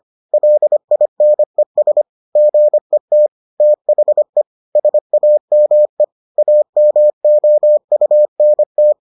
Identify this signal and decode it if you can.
Morse code, heard as a keyed tone, about 25 wpm, 600 Hz. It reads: LINES GET THE SAME AMOUNT